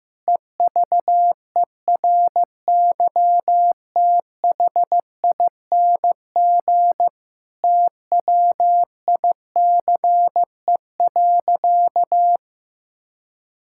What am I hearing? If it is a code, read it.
Morse code, heard as a keyed tone, about 16 wpm, 700 Hz. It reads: EVERYTHING TWICE.
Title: Sentences from Top 1600 Words 15wpm